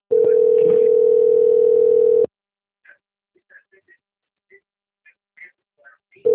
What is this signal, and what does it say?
Ringback tone (North America), 440 + 480 Hz.